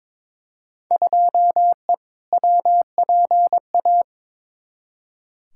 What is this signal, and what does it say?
Morse code, heard as a keyed tone, about 22 wpm, 700 Hz. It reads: 2E WPA
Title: Field Day Exchanges - 22wpm